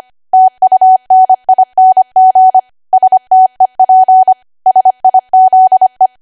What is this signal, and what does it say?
Morse code, heard as a keyed tone, about 25 wpm, 750 Hz. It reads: TUNING STEP SIZE
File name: Archive / 25 WPM / 25WPM